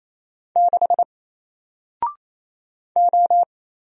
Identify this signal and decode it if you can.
Morse code, heard as a keyed tone, about 27 wpm, 700 Hz. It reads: 6 O
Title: Single Character (Letter:Number:Punctuation:Prosign) 28wpm